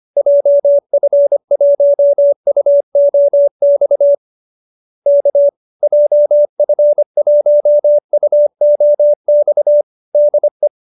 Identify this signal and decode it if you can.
Morse code, heard as a keyed tone, about 25 wpm, 570 Hz. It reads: JF1UOX K JF1UOX DE